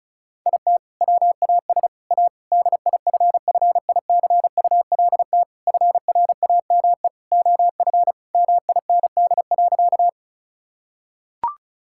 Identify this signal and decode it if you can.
Morse code, heard as a keyed tone, about 36 wpm, 700 Hz. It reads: IT WAS A DIFFICULT FRAME OF MIND.
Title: Sentences from Top 1000 Words - Review Part 6 - 35wpm